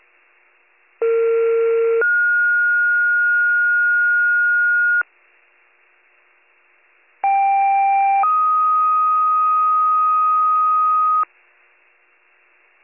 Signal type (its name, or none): two-tone page